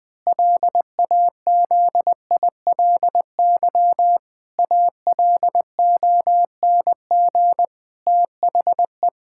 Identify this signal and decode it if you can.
Morse code, heard as a keyed tone, about 20 wpm, 700 Hz. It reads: LAZILY ALONG THE
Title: Pangram Challenge v2 - 20wpm